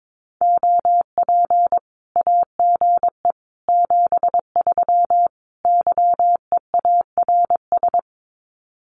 Morse code, heard as a keyed tone, about 22 wpm, 700 Hz. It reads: OP AGE 73 YEARS
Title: Common QSO Phrases 22wpm